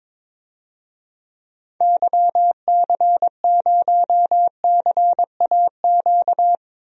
Morse code, heard as a keyed tone, about 22 wpm, 700 Hz. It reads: YC0CAQ
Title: DXCC-100 Call Signs - 6 Characters - Repeated 2x - 22wpm